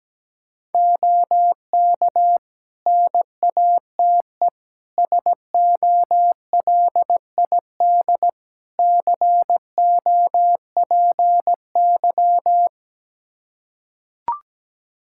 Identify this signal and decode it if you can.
Morse code, heard as a keyed tone, about 17 wpm, 700 Hz. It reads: OK NATE SOLID COPY
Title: Practice QSO - Exchange 3 - More Info - 17wpm